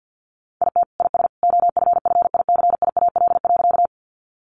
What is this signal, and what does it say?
Morse code, heard as a keyed tone, about 50 wpm, 700 Hz. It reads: IT IS OFFICIAL.
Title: Sentences from Top 1000 Words 50wpm